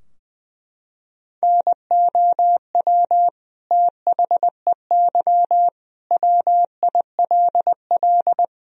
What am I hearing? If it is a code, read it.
Morse code, heard as a keyed tone, about 20 wpm, 700 Hz. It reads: NOW THEY WILL
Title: Sentences from Top 1000 Words - Review Part 4 - 20wpm